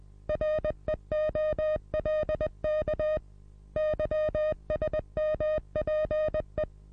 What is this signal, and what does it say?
Morse code, heard as a keyed tone, about 20 wpm, 620 Hz. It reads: REOLK YSMPE
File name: repos / Morsekurs / DJ4UF Morsen27